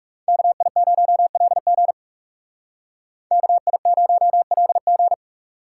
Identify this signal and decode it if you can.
Morse code, heard as a keyed tone, about 39 wpm, 700 Hz. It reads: KI0LG KI0LG